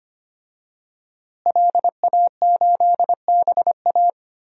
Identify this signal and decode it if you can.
Morse code, heard as a keyed tone, about 25 wpm, 700 Hz. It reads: LA8BA
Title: DXCC-100 Call Signs - 5 Characters - Repeated 2x - 25wpm